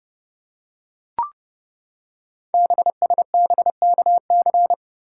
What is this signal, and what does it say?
Morse code, heard as a keyed tone, about 30 wpm, 700 Hz. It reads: BSBKC
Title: Code Groups - 5 Letters - 30wpm